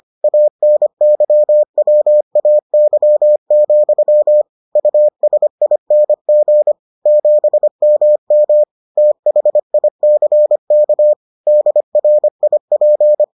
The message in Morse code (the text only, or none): ANYWAY, USING 7MM THICK DRIP